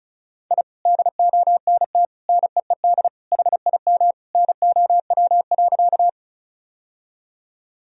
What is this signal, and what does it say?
Morse code, heard as a keyed tone, about 34 wpm, 700 Hz. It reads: I DONT NEED HIM NOW.
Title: Sentences from Top 1000 Words - Review Part 1 - 35wpm